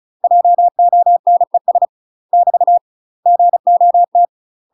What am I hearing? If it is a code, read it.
Morse code, heard as a keyed tone, about 36 wpm, 700 Hz. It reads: JONES <BT> GOT
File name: UK MASTER QSOs TRAINING / 35WPM / UK MASTER QSOs 0036